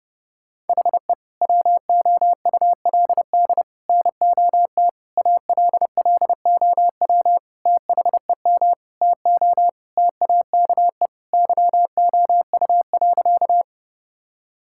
Morse code, heard as a keyed tone, about 30 wpm, 700 Hz. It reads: HE WOULD NOT ALLOW THEM TO TAKE YOU.
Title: Sentences from Top 1000 Words - Review Part 5 - 30wpm